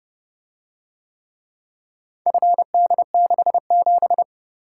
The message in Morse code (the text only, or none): FD67